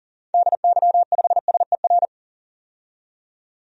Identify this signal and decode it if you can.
Morse code, heard as a keyed tone, about 40 wpm, 700 Hz. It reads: DY5SER
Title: DXCC-100 Call Signs - 6 Characters - 40wpm